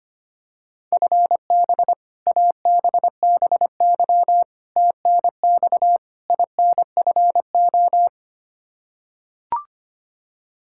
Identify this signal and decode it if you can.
Morse code, heard as a keyed tone, about 25 wpm, 700 Hz. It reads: FB ABBY TNX INFO
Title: Practice QSO - Exchange 3 - More Info - 25wpm